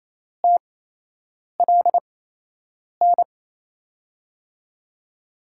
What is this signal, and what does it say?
Morse code, heard as a keyed tone, about 28 wpm, spaced as for 8 wpm, 700 Hz. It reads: TLN
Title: USA Call Signs - 5 Characters - 8x Character Spacing - 28wpm